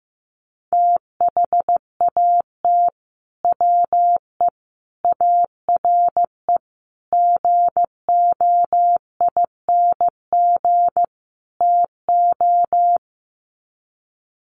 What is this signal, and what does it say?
Morse code, heard as a keyed tone, about 15 wpm, 700 Hz. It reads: THAT WE ARE GOING TO
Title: Sets of 5 Words Encore 15wpm